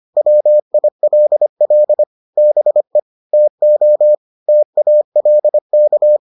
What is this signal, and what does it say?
Morse code, heard as a keyed tone, about 25 wpm, 600 Hz. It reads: WILL BE TO TALK